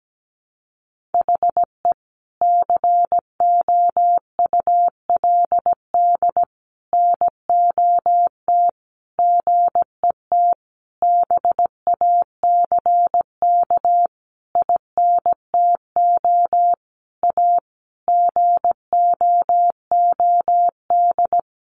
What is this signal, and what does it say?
Morse code, heard as a keyed tone, about 17 wpm, 700 Hz. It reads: HE COULD NOT GET BACK INTO A GOOD
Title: Sentences from Top 1100 Words 17wpm